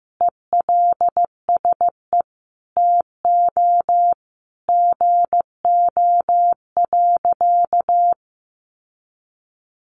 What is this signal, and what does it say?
Morse code, heard as a keyed tone, about 15 wpm, 700 Hz. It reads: ELSE TO GO.
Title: Sentences from Top 1400 Words 15wpm